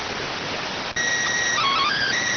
SSTV (slow-scan television), PD 290.